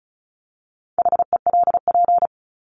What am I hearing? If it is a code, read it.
Morse code, heard as a keyed tone, about 34 wpm, 700 Hz. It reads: HELP
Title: Top 200 Words 35wpm